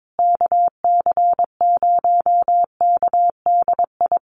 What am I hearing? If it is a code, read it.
Morse code, heard as a keyed tone, about 22 wpm, 700 Hz. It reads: KC0KDI
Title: USA Call Signs Repeated 3x 22wpm